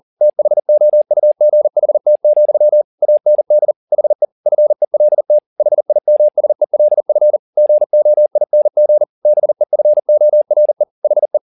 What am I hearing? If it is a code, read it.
Morse code, heard as a keyed tone, about 40 wpm, 600 Hz. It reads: THOUGHT, AND HE FELT HIMSELF GOING BEFORE HE
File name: the old man and the sea 2 40wpm